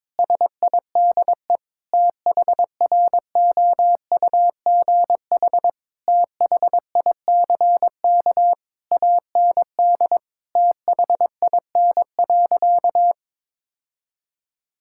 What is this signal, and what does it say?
Morse code, heard as a keyed tone, about 22 wpm, 700 Hz. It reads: SIDE THROUGH THICK AND THIN.
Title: Sentences from Top 1600 Words 22wpm